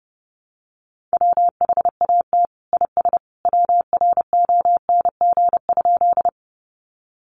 Morse code, heard as a keyed tone, about 30 wpm, 700 Hz. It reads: WHAT IS WRONG?
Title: Sentences from Top 900 Words 30wpm